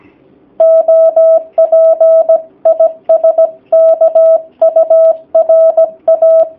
Morse code, heard as a keyed tone, about 17 wpm, 650 Hz. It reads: OP IS KURA
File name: jh8cbh / 2019cw / sp16- w